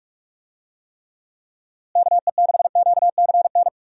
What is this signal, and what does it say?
Morse code, heard as a keyed tone, about 41 wpm, 700 Hz. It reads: KE6QCN
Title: USA Call Signs - 6 Characters - Repeated 2x - 45wpm